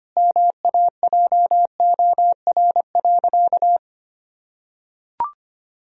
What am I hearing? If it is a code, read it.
Morse code, heard as a keyed tone, about 25 wpm, 700 Hz. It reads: MAJOR.
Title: Sentences from Top 1000 Words - Review Part 4 - 25wpm